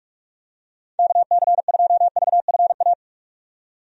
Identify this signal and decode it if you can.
Morse code, heard as a keyed tone, about 46 wpm, 700 Hz. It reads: KC2VFA
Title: USA Call Signs - 6 Characters - 45wpm